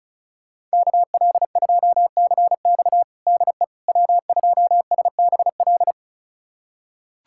Morse code, heard as a keyed tone, about 36 wpm, 700 Hz. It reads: KL2CX DE W2SBL
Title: USA Callsign DE Callsign - 5 Characters - 35wpm